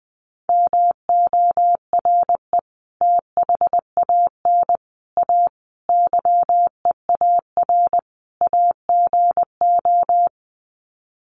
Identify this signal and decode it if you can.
Morse code, heard as a keyed tone, about 20 wpm, 700 Hz. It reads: MORE THAN A YEAR AGO
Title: Sets of 5 Words Encore2 20wpm